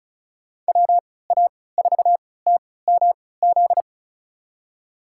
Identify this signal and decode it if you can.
Morse code, heard as a keyed tone, about 34 wpm, spaced as for 20 wpm, 700 Hz. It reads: WA4TMZ
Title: USA Call Signs - 6 Characters - 3x Character Spacing - 35wpm